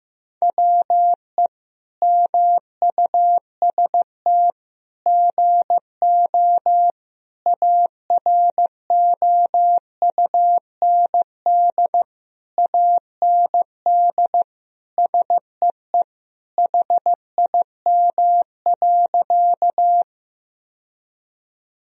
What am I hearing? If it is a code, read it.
Morse code, heard as a keyed tone, about 15 wpm, 700 Hz. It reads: WE MUST GO AROUND AND SEE HIM.
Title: Sentences from Top 1000 Words - Review Part 3 - 15wpm